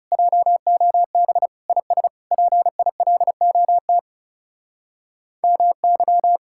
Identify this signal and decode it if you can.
Morse code, heard as a keyed tone, about 39 wpm, 700 Hz. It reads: JOB IS PILOT MY